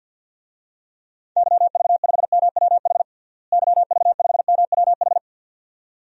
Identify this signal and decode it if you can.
Morse code, heard as a keyed tone, about 52 wpm, 700 Hz. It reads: YV5MWH YV5MWH
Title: DXCC-100 Call Signs - 6 Characters - Repeated 2x - 50wpm